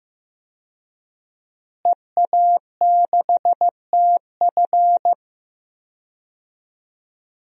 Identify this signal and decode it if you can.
Morse code, heard as a keyed tone, about 15 wpm, 700 Hz. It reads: EA6TF
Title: DXCC-100 Call Signs - 5 Characters - Repeated 3x - 15wpm